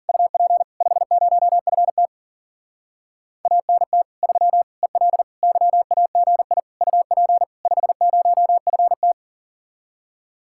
Morse code, heard as a keyed tone, about 48 wpm, 700 Hz. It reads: UP 50FT ANT 3 EL YAGI UP 50FT
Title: Speed-Racing Common QSO Phrases - 40wpm